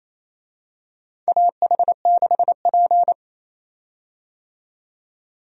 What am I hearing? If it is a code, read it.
Morse code, heard as a keyed tone, about 28 wpm, 700 Hz. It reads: AH6P
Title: USA Call Signs - 4 Characters - 28wpm